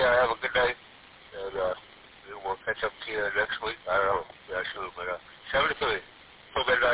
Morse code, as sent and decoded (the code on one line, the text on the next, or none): none